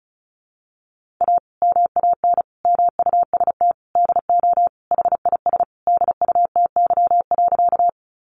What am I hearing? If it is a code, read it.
Morse code, heard as a keyed tone, about 34 wpm, 700 Hz. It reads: A MAN MUST DO HIS DUTY.
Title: Sentences from Top 1500 Words 35wpm